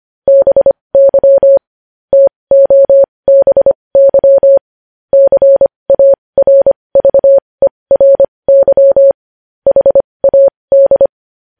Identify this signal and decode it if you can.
Morse code, heard as a keyed tone, about 25 wpm, spaced as for 20 wpm, 550 Hz. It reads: BY TOBY CARVERY HAD
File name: podcast / News headlines-1x-25x20